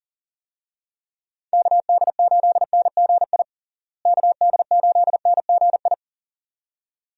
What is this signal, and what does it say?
Morse code, heard as a keyed tone, about 40 wpm, 700 Hz. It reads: KD8NGI KD8NGI
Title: USA Call Signs - 6 Characters - Repeated 2x - 40wpm